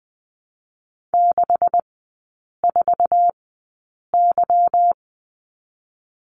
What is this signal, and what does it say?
Morse code, heard as a keyed tone, about 20 wpm, 700 Hz. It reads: 6 4 Y
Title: 198 Lesson 32e - Letter k - Three Characters 20wpm